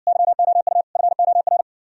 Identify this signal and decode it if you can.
Morse code, heard as a keyed tone, about 60 wpm, 700 Hz. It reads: YOU FOR